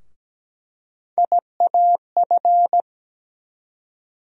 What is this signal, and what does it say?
Morse code, heard as a keyed tone, about 17 wpm, 700 Hz. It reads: IAF
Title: USA Call Sign Postfixes - 3 Letters 17wpm